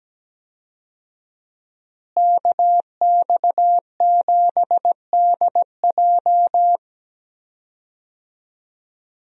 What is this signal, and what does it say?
Morse code, heard as a keyed tone, about 17 wpm, 700 Hz. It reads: KX7DJ
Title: USA Call Signs - 5 Characters - Repeated 3x - 17wpm